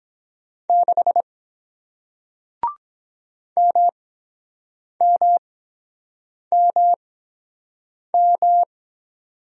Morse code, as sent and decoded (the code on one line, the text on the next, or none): -.... / -- -- -- --
6 MMMM